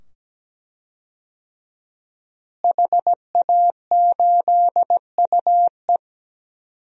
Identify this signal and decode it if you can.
Morse code, heard as a keyed tone, about 17 wpm, 700 Hz. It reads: HA8UE
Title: DXCC-100 Call Signs - 5 Characters - Repeated 2x - 17wpm